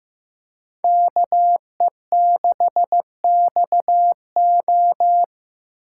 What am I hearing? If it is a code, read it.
Morse code, heard as a keyed tone, about 19 wpm, 700 Hz. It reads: KE6XO